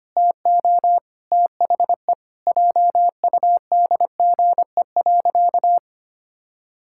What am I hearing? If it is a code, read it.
Morse code, heard as a keyed tone, about 25 wpm, 700 Hz. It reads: TO THE JUDGE.
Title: Pangram Challenge v2 - 25wpm